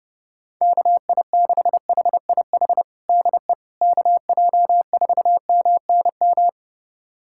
Morse code, heard as a keyed tone, about 31 wpm, 700 Hz. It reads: KI6HIH DE KJ4MNM
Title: USA Callsign DE Callsign 30wpm